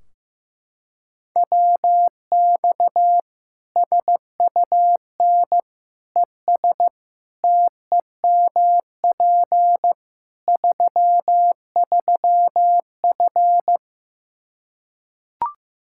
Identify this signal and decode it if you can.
Morse code, heard as a keyed tone, about 15 wpm, 700 Hz. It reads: WX SUN ES TEMP 33F
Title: K1USN SST QSO Practice 15wpm